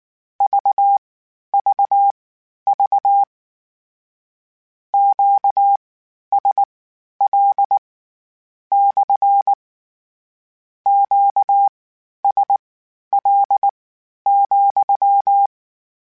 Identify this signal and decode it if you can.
Morse code, heard as a keyed tone, about 19 wpm, spaced as for 11 wpm, 800 Hz. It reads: VVV QSL / QSL,
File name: Q Calls / Q Calls-mp3s / Q Calls SPSL2 @18